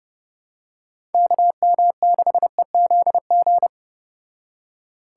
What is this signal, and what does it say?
Morse code, heard as a keyed tone, about 30 wpm, 700 Hz. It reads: KM6EZG